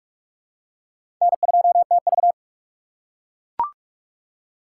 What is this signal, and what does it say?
Morse code, heard as a keyed tone, about 43 wpm, 700 Hz. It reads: N2TV
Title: USA Call Signs - 6x Character Spacing - 45wpm